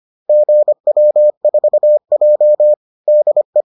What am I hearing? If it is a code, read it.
Morse code, heard as a keyed tone, about 25 wpm, 600 Hz. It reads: GW4J DE